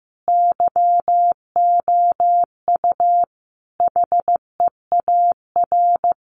Morse code, heard as a keyed tone, about 15 wpm, 700 Hz. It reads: YOU HEAR
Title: Sentences from Top 300 Words 15wpm